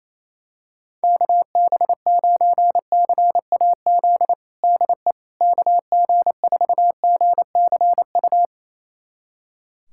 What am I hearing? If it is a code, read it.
Morse code, heard as a keyed tone, about 28 wpm, 700 Hz. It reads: KB9CAZ DE KG4GCU
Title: USA Callsign DE Callsign 28wpm